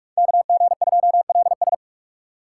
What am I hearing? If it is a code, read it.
Morse code, heard as a keyed tone, about 44 wpm, 700 Hz. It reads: KG2LS